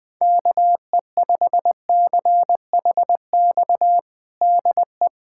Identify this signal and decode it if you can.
Morse code, heard as a keyed tone, about 20 wpm, 700 Hz. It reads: KE5CHX DE
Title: USA Callsign DE Callsign - 6 Characters - 20wpm